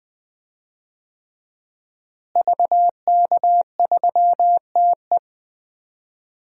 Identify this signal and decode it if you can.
Morse code, heard as a keyed tone, about 20 wpm, 700 Hz. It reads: VK3TE
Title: DXCC-100 Call Signs - 5 Characters - Repeated 3x - 20wpm